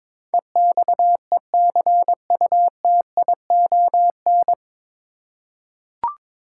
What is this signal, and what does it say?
Morse code, heard as a keyed tone, about 21 wpm, 700 Hz. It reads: EXECUTION
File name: 9 Letter Words - First 3 Letters Given - 22wpm